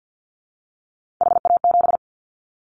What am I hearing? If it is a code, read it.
Morse code, heard as a keyed tone, about 50 wpm, 700 Hz. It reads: HA7
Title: Callsign Prefixes for Top 50 DXCC - 50wpm